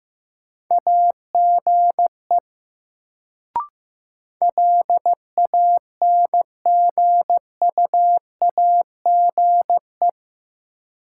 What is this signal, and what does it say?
Morse code, heard as a keyed tone, about 15 wpm, 700 Hz. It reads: AGE LANGUAGE
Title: Sentences from Top 500 Words - New Word Drill - 15wpm